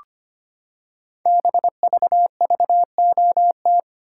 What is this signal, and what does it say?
Morse code, heard as a keyed tone, about 25 wpm, 700 Hz. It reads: BVVOT
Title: Code Groups - 5 Letter:Numbers - 25wpm